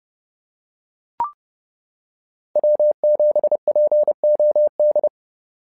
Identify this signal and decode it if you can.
Morse code, heard as a keyed tone, about 30 wpm, 590 Hz. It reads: W7POD
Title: SOTA QSO Practice 30wpm